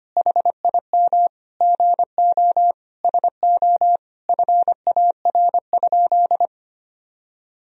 Morse code, heard as a keyed tone, about 25 wpm, 700 Hz. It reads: HIM GO SO FAR?
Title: Sentences from Top 300 Words and Abbreviations 25wpm